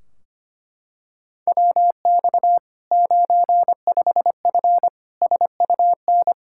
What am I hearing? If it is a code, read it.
Morse code, heard as a keyed tone, about 25 wpm, 700 Hz. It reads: WX 95F SUN
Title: Practice QSO - Exchange 2 - Info - 25wpm